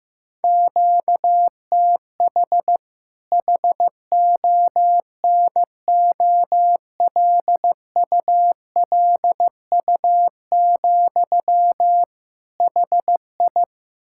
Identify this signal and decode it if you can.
Morse code, heard as a keyed tone, about 15 wpm, 700 Hz. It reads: QTH HONOLULU, HI
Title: K1USN SST QSO Practice 15wpm